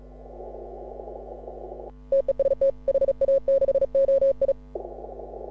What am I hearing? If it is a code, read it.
Morse code, heard as a keyed tone, about 37 wpm, 530 Hz. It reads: TEST HA6OI
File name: AUDIO2018 / HA6OI CW